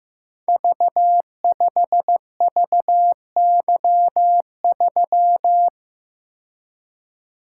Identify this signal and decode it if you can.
Morse code, heard as a keyed tone, about 15 wpm, 700 Hz. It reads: V5VY3
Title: Code Groups - 5 Letter:Numbers - 15wpm